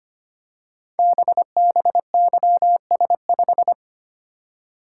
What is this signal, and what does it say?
Morse code, heard as a keyed tone, about 25 wpm, 700 Hz. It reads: BBYS5
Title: Code Groups - 5 Letter:Numbers - 25wpm